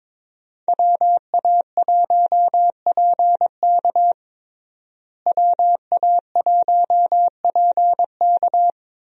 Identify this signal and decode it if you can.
Morse code, heard as a keyed tone, about 22 wpm, 700 Hz. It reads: WA1PK WA1PK
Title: USA Call Signs - 5 Characters - Repeated 2x - 22wpm